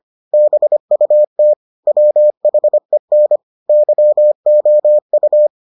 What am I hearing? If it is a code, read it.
Morse code, heard as a keyed tone, about 25 wpm, 600 Hz. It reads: BUT WHEN YOU